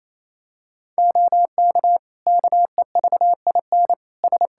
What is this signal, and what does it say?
Morse code, heard as a keyed tone, about 28 wpm, 700 Hz. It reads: OK KEVIN S
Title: Practice QSO - Exchange 3 - More Info - 28wpm